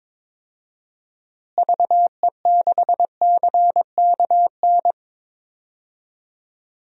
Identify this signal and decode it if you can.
Morse code, heard as a keyed tone, about 22 wpm, 700 Hz. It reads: VE6CKN
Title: DXCC-100 Call Signs - 6 Characters - Repeated 3x - 22wpm